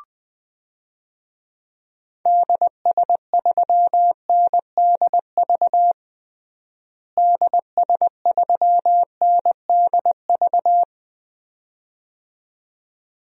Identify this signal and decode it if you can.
Morse code, heard as a keyed tone, about 20 wpm, 700 Hz. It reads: DS3NDV DS3NDV
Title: DXCC-100 Call Signs - 6 Characters - Repeated 2x - 20wpm